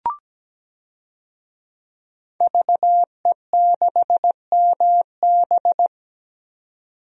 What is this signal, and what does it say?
Morse code, heard as a keyed tone, about 17 wpm, 700 Hz. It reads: VE6MB
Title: DXCC-100 Call Signs - 5 Characters - 17wpm